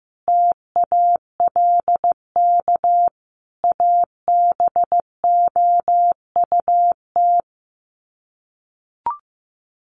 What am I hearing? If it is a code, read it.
Morse code, heard as a keyed tone, about 15 wpm, 700 Hz. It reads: TALK ABOUT
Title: Sets of 5 Words 15wpm